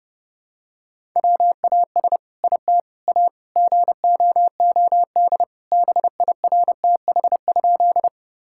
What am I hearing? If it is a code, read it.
Morse code, heard as a keyed tone, about 30 wpm, 700 Hz. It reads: WAS IT A GOOD BIRTH?
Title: Sentences from Top 1000 Words - Review Part 5 - 30wpm